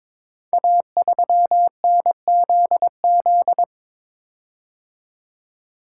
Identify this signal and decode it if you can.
Morse code, heard as a keyed tone, about 23 wpm, 700 Hz. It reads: A3NZZ
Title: Code Groups - 5 Letter:Numbers - 22wpm